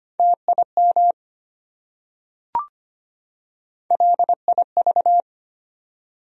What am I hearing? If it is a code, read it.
Morse code, heard as a keyed tone, about 25 wpm, 700 Hz. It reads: TIM LIV
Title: USA Common Names - Max 3 Letters - 25wpm